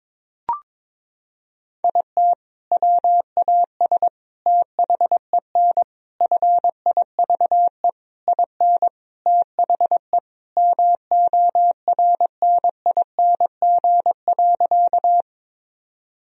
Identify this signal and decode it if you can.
Morse code, heard as a keyed tone, about 21 wpm, 700 Hz. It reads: IT WAS THEN FIVE IN THE MORNING.
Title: Sentences from Top 1000 Words - Review Part 2 - 22wpm